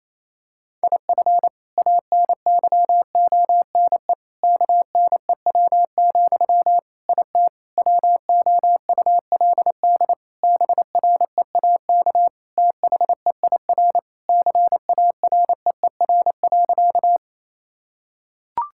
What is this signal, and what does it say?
Morse code, heard as a keyed tone, about 28 wpm, 700 Hz. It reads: IF ANYONE KNEW, IT WOULD BREAK THEIR CAREER.
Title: Sentences from Top 1000 Words - Review Part 6 - 28wpm